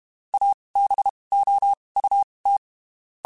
Morse code, heard as a keyed tone, about 32 wpm, spaced as for 23 wpm, 800 Hz. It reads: ABOUT
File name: (spaced) / QSOs Spaced-mp3s / WD0BGP 1 @30 sp